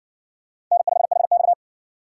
Morse code, heard as a keyed tone, about 54 wpm, 700 Hz. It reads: N5HX